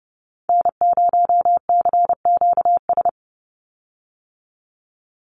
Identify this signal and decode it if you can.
Morse code, heard as a keyed tone, about 30 wpm, 700 Hz. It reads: N0CQS
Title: USA Call Signs 30wpm